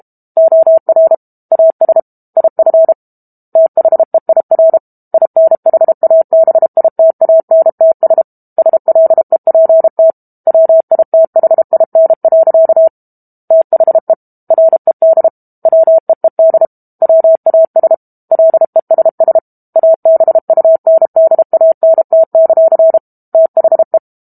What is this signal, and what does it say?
Morse code, heard as a keyed tone, about 32 wpm, 650 Hz. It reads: OR AS IF THEIR INHABITANTS SLEPT WITHIN. THE RED WEED WAS LESS ABUNDANT; THE
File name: Book Two / 06-Chapter 6 - 32 WPM